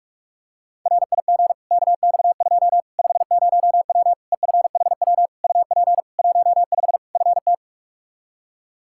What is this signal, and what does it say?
Morse code, heard as a keyed tone, about 42 wpm, 700 Hz. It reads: RIG KX2 50W EFHW UP 15 FT
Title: K1USN SST QSO Practice 45wpm